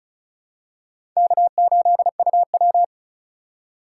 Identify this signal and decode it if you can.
Morse code, heard as a keyed tone, about 36 wpm, 700 Hz. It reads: K8UW
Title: USA Call Signs - 3x Character Spacing - 35wpm